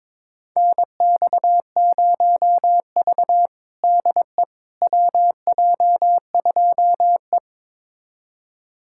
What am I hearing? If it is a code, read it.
Morse code, heard as a keyed tone, about 22 wpm, 700 Hz. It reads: NX0V DE WJ2E
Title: USA Callsign DE Callsign - 4 Characters - 22wpm